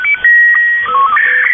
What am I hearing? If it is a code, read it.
SSTV (slow-scan television), Scottie 1.
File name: mw / 0256 6950u wolverine radio inc sstv